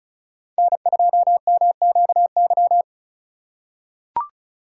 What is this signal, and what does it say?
Morse code, heard as a keyed tone, about 34 wpm, 700 Hz. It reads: N2MQY
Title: USA Call Signs - 5 Characters - 4x Character Spacing - 35wpm